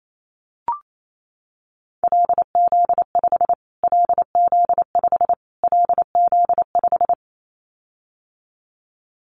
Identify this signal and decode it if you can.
Morse code, heard as a keyed tone, about 28 wpm, 700 Hz. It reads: LZ5 LZ5 LZ5
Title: Callsign Prefixes for Top 25 DXCC - Repeated 3x - 28wpm